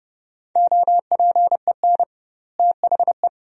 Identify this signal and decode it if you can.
Morse code, heard as a keyed tone, about 30 wpm, 700 Hz. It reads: OPEN THE
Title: 2x Word-Spacing - Sentences from Top 300 Words - 30wpm